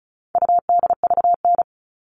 Morse code, heard as a keyed tone, about 34 wpm, 700 Hz. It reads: UDVN
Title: All Cut Numbers - 4-Digit 35wpm